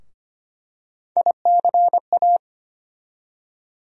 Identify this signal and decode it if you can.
Morse code, heard as a keyed tone, about 25 wpm, 700 Hz. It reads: ICA
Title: USA Call Sign Postfixes - 3 Letters 25wpm